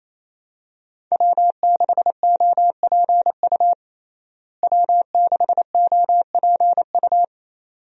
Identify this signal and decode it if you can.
Morse code, heard as a keyed tone, about 27 wpm, 700 Hz. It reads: W6OPU W6OPU
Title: USA Call Signs - 5 Characters - Repeated 2x - 28wpm